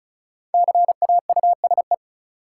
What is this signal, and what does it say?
Morse code, heard as a keyed tone, about 36 wpm, 700 Hz. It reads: CAUSE
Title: Sentences from Top 200 Words - New Word Drill - 35wpm